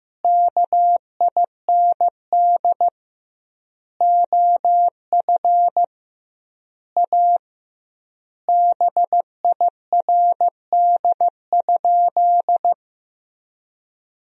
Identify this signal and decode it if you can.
Morse code, heard as a keyed tone, about 15 wpm, 700 Hz. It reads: KIND OF A BIRD?
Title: 2x Word-Spacing - Sentences from Top 400 Words - 15wpm